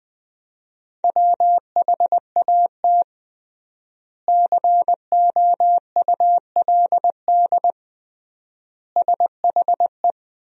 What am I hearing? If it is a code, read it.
Morse code, heard as a keyed tone, about 20 wpm, 700 Hz. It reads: WHAT COULD SHE
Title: 3x Word-Spacing - Sentences from Top 200 Words - 20wpm